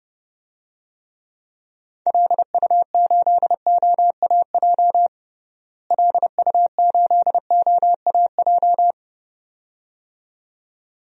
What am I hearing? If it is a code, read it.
Morse code, heard as a keyed tone, about 32 wpm, 700 Hz. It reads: LU8OAJ LU8OAJ
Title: DXCC-100 Call Signs - 6 Characters - Repeated 2x - 30wpm